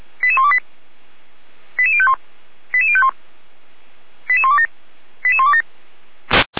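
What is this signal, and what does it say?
Five-tone selective call (ZVEI), heard as quick runs of five tones, tones of about 80 ms. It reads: 88128 88051 88051 88127 88127
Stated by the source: ZVEI five-tone selective call 88129 received by SDR.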